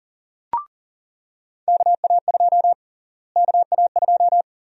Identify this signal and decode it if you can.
Morse code, heard as a keyed tone, about 40 wpm, 700 Hz. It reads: KA2 KA2
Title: USA Call Sign Prefixes - Repeated 2x - 40wpm